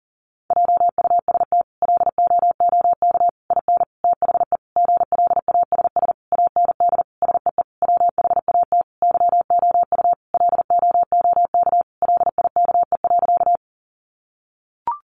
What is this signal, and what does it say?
Morse code, heard as a keyed tone, about 40 wpm, 700 Hz. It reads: JUST LOOK IN THE GLASS AND SEE WHAT YOU LOOK LIKE.
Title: Sentences from Top 1300 Words 40wpm